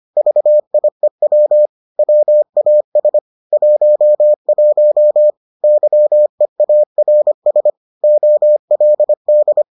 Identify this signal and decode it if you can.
Morse code, heard as a keyed tone, about 25 wpm, 600 Hz. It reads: VIEW WAS 11 YEARS OLD